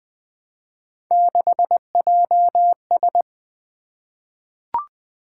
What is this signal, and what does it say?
Morse code, heard as a keyed tone, about 20 wpm, 700 Hz. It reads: 6JS
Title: Three Letter-Number Combinations - 20wpm